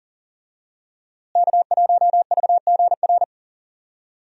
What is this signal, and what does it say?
Morse code, heard as a keyed tone, about 40 wpm, 700 Hz. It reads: K1VGR
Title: USA Call Signs - 5 Characters - Repeated 2x - 40wpm